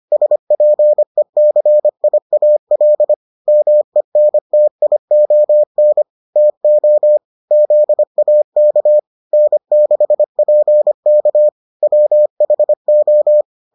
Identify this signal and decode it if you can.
Morse code, heard as a keyed tone, about 25 wpm, 600 Hz. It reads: SPECIAL MENTION TO ZAK N6PK WHO